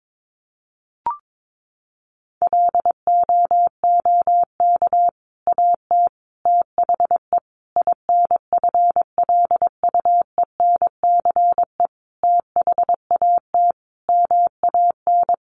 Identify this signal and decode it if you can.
Morse code, heard as a keyed tone, about 22 wpm, 700 Hz. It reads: LOOK AT THE INFLUENCE THAT MAN
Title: Sentences from Top 1100 Words 22wpm